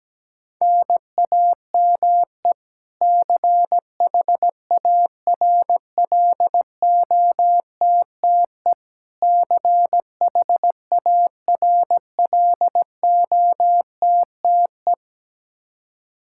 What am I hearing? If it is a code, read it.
Morse code, heard as a keyed tone, about 17 wpm, 700 Hz. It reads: NAME CHARLOTTE CHARLOTTE
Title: Practice QSO - Exchange 1 - Beginning - 17wpm